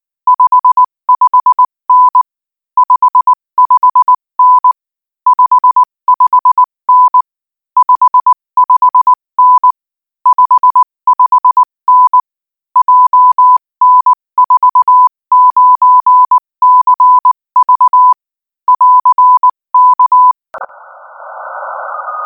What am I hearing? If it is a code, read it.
Morse code, heard as a keyed tone, about 19 wpm, 1000 Hz. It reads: 55N 55N 55N 55N 55N JN49CV <AR> K
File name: IZ4BEH-13-cm-AS